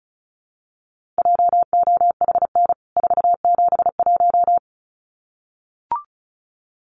Morse code, heard as a keyed tone, about 34 wpm, 700 Hz. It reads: JOHN 471
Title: CWT Exchange 35wpm